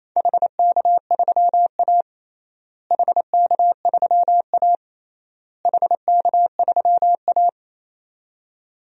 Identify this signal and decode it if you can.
Morse code, heard as a keyed tone, about 28 wpm, 700 Hz. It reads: HK3A HK3A HK3A
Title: DXCC-100 Call Signs - 4 Characters - Repeated 3x - 28wpm